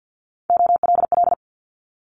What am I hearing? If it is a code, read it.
Morse code, heard as a keyed tone, about 50 wpm, 700 Hz. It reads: OLL
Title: Multi-letter Phonemes Quick Fire 50wpm